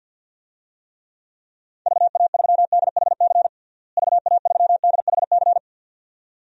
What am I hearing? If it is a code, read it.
Morse code, heard as a keyed tone, about 52 wpm, 700 Hz. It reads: VA3DHC VA3DHC
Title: DXCC-100 Call Signs - 6 Characters - Repeated 2x - 50wpm